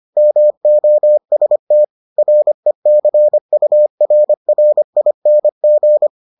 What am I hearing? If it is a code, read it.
Morse code, heard as a keyed tone, about 25 wpm, 600 Hz. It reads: MOST RECURRING